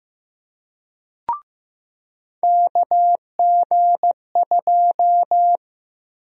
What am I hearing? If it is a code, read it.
Morse code, heard as a keyed tone, about 15 wpm, 700 Hz. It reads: KG2
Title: USA Call Sign Prefixes - Repeated 2x - 15wpm